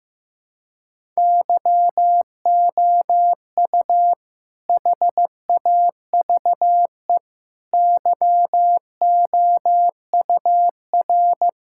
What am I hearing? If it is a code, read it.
Morse code, heard as a keyed tone, about 15 wpm, 700 Hz. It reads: YOU HAVE YOUR
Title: Sentences from Top 1000 Words - Review Part 5 - 15wpm